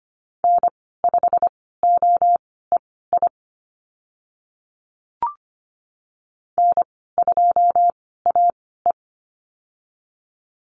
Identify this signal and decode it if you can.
Morse code, heard as a keyed tone, about 25 wpm, spaced as for 16 wpm, 700 Hz. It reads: N5OEI N2AE
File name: Lesson 15g - Number 5 - Call Signs 25wpm